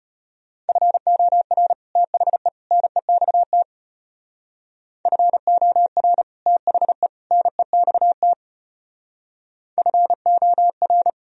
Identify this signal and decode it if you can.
Morse code, heard as a keyed tone, about 32 wpm, 700 Hz. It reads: FOR THE NEXT FOR THE NEXT FOR